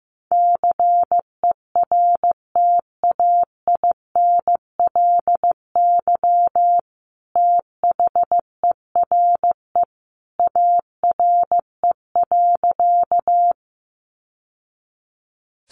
Morse code, heard as a keyed tone, about 15 wpm, 700 Hz. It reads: CERTAINLY THERE ARE.
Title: Sentences from Top 900 Words 15wpm